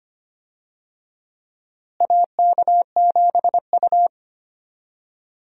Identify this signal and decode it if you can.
Morse code, heard as a keyed tone, about 25 wpm, 700 Hz. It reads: AK7U
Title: USA Call Signs - 4 Characters - Repeated 2x - 25wpm